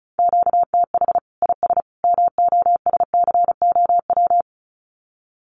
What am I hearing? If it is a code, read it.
Morse code, heard as a keyed tone, about 34 wpm, 700 Hz. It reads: QTH IS MOSCOW
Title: Common QSO Phrases 35wpm